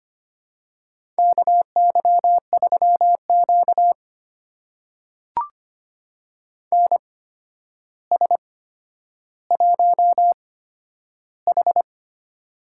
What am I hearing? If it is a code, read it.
Morse code, heard as a keyed tone, about 25 wpm, spaced as for 11 wpm, 700 Hz. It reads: KY3Q NS1H
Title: USA Call Signs - 4 Characters - 8x Character Spacing - 25wpm